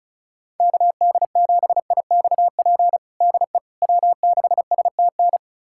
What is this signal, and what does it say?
Morse code, heard as a keyed tone, about 36 wpm, 700 Hz. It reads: KD7IXP DE W6STN
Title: USA Callsign DE Callsign 35wpm